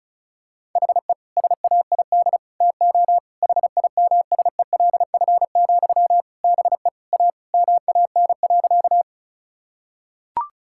Morse code, heard as a keyed tone, about 36 wpm, 700 Hz. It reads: HE SAID TO HIMSELF, BE A MAN.
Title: Sentences from Top 1600 Words 35wpm